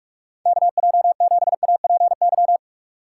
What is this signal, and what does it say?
Morse code, heard as a keyed tone, about 46 wpm, 700 Hz. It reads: KJ7APY
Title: USA Call Signs - 6 Characters - 2x Character Spacing - 45wpm